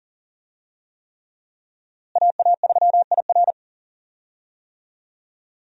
Morse code, heard as a keyed tone, about 40 wpm, 700 Hz. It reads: AA3IR
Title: USA Call Signs - 5 Characters - Repeated 3x - 40wpm